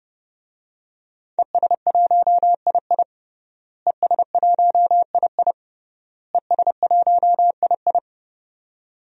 Morse code, heard as a keyed tone, about 30 wpm, 700 Hz. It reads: ES1II ES1II ES1II
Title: DXCC-100 Call Signs - 5 Characters - Repeated 3x - 30wpm